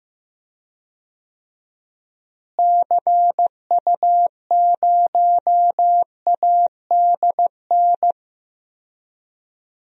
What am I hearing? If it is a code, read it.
Morse code, heard as a keyed tone, about 15 wpm, 700 Hz. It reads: CU0ADN
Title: DXCC-100 Call Signs - 6 Characters - Repeated 3x - 15wpm